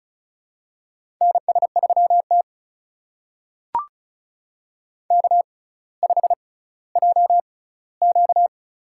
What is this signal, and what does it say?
Morse code, heard as a keyed tone, about 36 wpm, spaced as for 18 wpm, 700 Hz. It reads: NS3T K5JQ
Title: USA Call Signs - 4 Characters - 6x Character Spacing - 35wpm